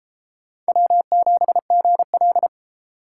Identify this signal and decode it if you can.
Morse code, heard as a keyed tone, about 29 wpm, 700 Hz. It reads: W7GL